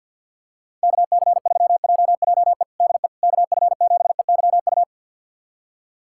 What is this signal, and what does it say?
Morse code, heard as a keyed tone, about 50 wpm, 700 Hz. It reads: KC3JJE DE KF7EYU
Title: USA Callsign DE Callsign 50wpm